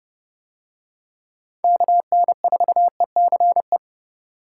Morse code, heard as a keyed tone, about 29 wpm, 700 Hz. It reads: KN4ECE